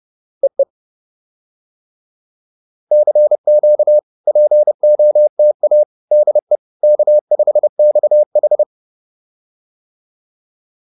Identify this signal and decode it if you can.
Morse code, heard as a keyed tone, about 30 wpm, 600 Hz. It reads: EE CQ POTA DE K5XH